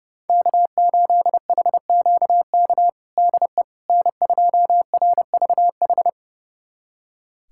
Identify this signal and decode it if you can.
Morse code, heard as a keyed tone, about 32 wpm, 700 Hz. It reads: K8HQK DE N2RVH
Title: USA Callsign DE Callsign - 5 Characters - 30wpm